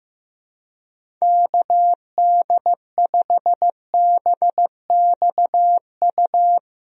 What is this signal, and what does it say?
Morse code, heard as a keyed tone, about 15 wpm, 700 Hz. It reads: KD5BXU
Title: USA Call Signs - 6 Characters - Repeated 3x - 15wpm